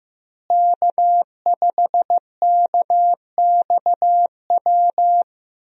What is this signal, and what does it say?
Morse code, heard as a keyed tone, about 15 wpm, 700 Hz. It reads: K5KXW
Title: USA Call Signs - 5 Characters - Repeated 3x - 15wpm